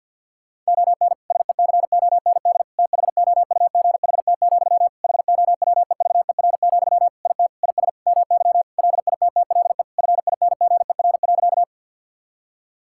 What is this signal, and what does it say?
Morse code, heard as a keyed tone, about 50 wpm, 700 Hz. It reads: ON SECOND THOUGHT, HOWEVER, IT IS MY LITTLE FINGER.
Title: Sentences from Top 1600 Words 50wpm